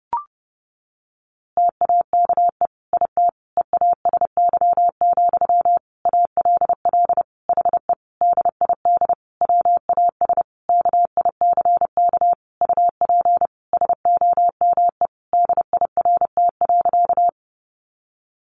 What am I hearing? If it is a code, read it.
Morse code, heard as a keyed tone, about 30 wpm, 700 Hz. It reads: TAKE IT EASY, ALL HE DID WAS KICK UP SOME DIRT.
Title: A Sound Of Thunder 30wpm